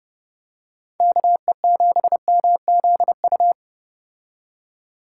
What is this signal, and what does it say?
Morse code, heard as a keyed tone, about 30 wpm, 700 Hz. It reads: KE7MZU